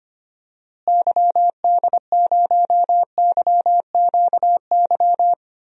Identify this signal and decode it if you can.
Morse code, heard as a keyed tone, about 25 wpm, 700 Hz. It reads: YD0YQY
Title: DXCC-100 Call Signs - 6 Characters - Repeated 2x - 25wpm